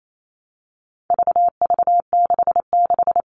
Morse code, heard as a keyed tone, about 28 wpm, 700 Hz. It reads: VV66
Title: All Cut Numbers - 4-Digit 28wpm